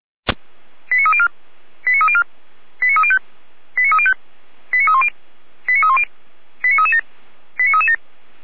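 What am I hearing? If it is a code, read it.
Five-tone selective call (ZVEI), heard as quick runs of five tones, tones of about 70 ms. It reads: 89304 89304 89305 89305 89310 89310 89337 89337
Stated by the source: ZVEI five-tone selective call received by SDR.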